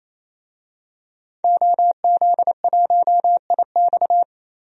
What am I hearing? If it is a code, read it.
Morse code, heard as a keyed tone, about 28 wpm, 700 Hz. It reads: OZ1IX
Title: DXCC-100 Call Signs - 5 Characters - 28wpm